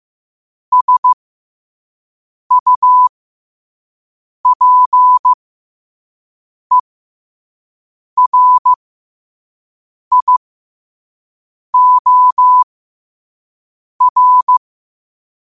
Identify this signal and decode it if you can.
Morse code, heard as a keyed tone, about 15 wpm, spaced as for 5 wpm, 1000 Hz. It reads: SUPERIOR
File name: qotdfiles / 5wpm